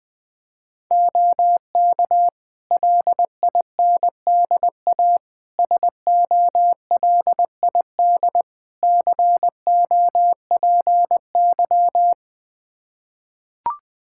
Morse code, heard as a keyed tone, about 20 wpm, 700 Hz. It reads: OK LINDA SOLID COPY
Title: Practice QSO - Exchange 3 - More Info - 20wpm